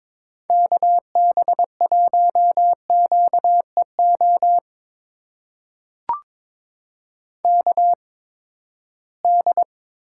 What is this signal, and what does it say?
Morse code, heard as a keyed tone, about 21 wpm, spaced as for 15 wpm, 700 Hz. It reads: KB1QEO KD
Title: USA Call Signs - 6 Characters - 8x Character Spacing - 22wpm